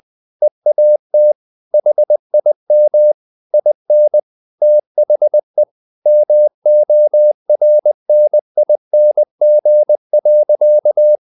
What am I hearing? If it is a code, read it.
Morse code, heard as a keyed tone, about 20 wpm, 600 Hz. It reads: EAT HIM IN THE MORNING.
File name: the old man and the sea / the old man and the sea part 1 20wpm